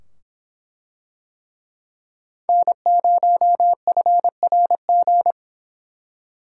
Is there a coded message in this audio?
Yes: Morse code from 2.5 to 5.3 s, heard as a keyed tone, about 24 wpm, 700 Hz. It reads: N0FRG